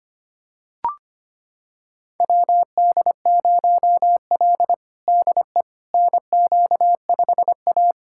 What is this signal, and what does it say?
Morse code, heard as a keyed tone, about 25 wpm, 700 Hz. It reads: WD0L DE NQ5A
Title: USA Callsign DE Callsign - 4 Characters - 25wpm